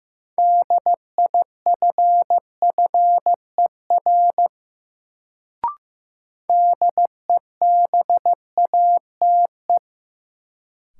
Morse code, heard as a keyed tone, about 15 wpm, 700 Hz. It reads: DIFFER DEBATE
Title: Sentences from Top 1100 Words - New Word Drill Alt - 15wpm